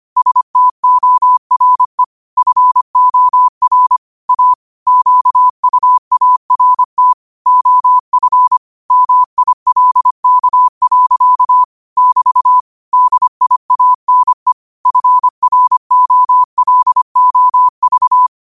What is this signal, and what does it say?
Morse code, heard as a keyed tone, about 25 wpm, 1000 Hz. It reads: STORE FOR A QUART OF MILK. <BT> DIANE FROLOV
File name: qotdfiles / 25wpm